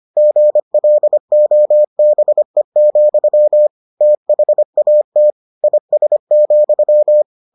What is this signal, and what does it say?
Morse code, heard as a keyed tone, about 25 wpm, 600 Hz. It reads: GLOBE, THAT IS,